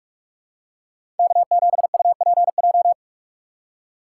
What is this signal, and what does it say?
Morse code, heard as a keyed tone, about 42 wpm, 700 Hz. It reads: K7UPJ
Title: K1USN SST QSO Practice - 45wpm